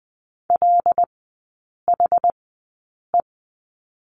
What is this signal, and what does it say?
Morse code, heard as a keyed tone, about 20 wpm, spaced as for 8 wpm, 700 Hz. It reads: LHE
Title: 074 Lesson 13e - Letter L - Three Characters 20wpm